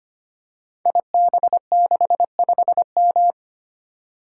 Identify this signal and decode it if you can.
Morse code, heard as a keyed tone, about 25 wpm, 700 Hz. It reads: IB65M
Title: Code Groups - 5 Letter:Numbers - 25wpm